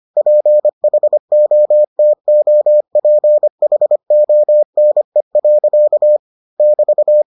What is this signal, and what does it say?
Morse code, heard as a keyed tone, about 25 wpm, 600 Hz. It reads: PHOTOPHONE. <BT>